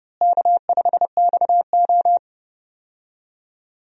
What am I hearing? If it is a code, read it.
Morse code, heard as a keyed tone, about 30 wpm, 700 Hz. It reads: K5XO
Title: USA Call Signs - 4 Characters - Repeated 3x - 30wpm